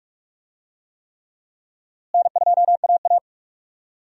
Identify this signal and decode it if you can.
Morse code, heard as a keyed tone, about 45 wpm, 700 Hz. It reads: N2AA
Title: USA Call Signs - 4 Characters - Repeated 3x - 45wpm